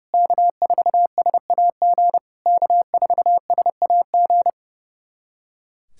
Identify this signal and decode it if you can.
Morse code, heard as a keyed tone, about 30 wpm, 700 Hz. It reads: K4SAG K4SAG
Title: SOTA QSO Practice 30wpm